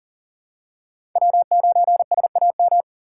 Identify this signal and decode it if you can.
Morse code, heard as a keyed tone, about 40 wpm, 700 Hz. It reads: W9SAM
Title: USA Call Signs - 5 Characters - 2x Character Spacing - 40wpm